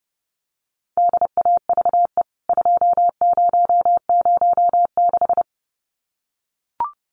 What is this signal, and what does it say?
Morse code, heard as a keyed tone, about 30 wpm, 700 Hz. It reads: DAVE 2006
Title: CWT Exchange 30wpm